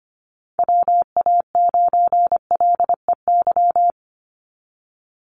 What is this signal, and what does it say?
Morse code, heard as a keyed tone, about 25 wpm, 700 Hz. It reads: WA9LEY
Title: Call Signs Repeated 2x - 25wpm